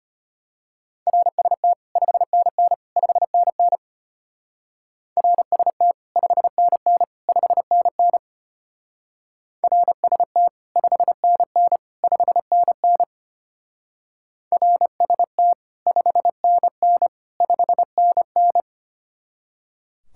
Morse code, heard as a keyed tone, about 30 wpm, 700 Hz. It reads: RST 5NN 5NN RST 5NN 5NN RST 5NN 5NN RST 5NN 5NN
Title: Speed-Racing Common QSO Phrases - 25wpm